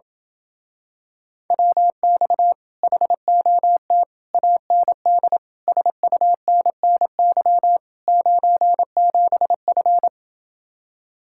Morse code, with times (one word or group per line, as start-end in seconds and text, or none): 1.5-2.5 WX
2.8-4.0 HOT
4.3-5.4 AND
5.7-7.8 SUNNY
8.1-10.1 97F